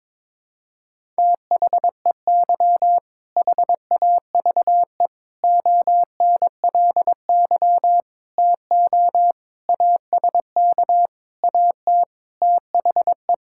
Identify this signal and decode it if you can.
Morse code, heard as a keyed tone, about 22 wpm, 700 Hz. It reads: THEY HAVE ONLY TO ASK AT THE
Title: Sentences from Top 1000 Words - Review Part 5 - 22wpm